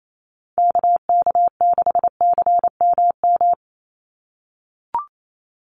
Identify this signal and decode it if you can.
Morse code, heard as a keyed tone, about 28 wpm, 700 Hz. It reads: KK6CMM
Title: USA Call Signs 28wpm